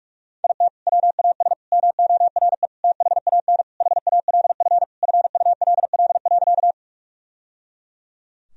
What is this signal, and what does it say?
Morse code, heard as a keyed tone, about 48 wpm, 700 Hz. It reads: IT WAS MORE THAN HALF FULL.
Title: Sentences from Top 1000 Words - Review Part 3 - 45wpm